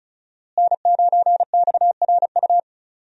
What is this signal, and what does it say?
Morse code, heard as a keyed tone, about 36 wpm, 700 Hz. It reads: N9XRU
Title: USA Call Signs - 5 Characters - Repeated 3x - 35wpm